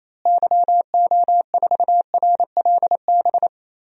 Morse code, heard as a keyed tone, about 28 wpm, 700 Hz. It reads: YO4RLB
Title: DXCC-100 Call Signs - 6 Characters - Repeated 2x - 28wpm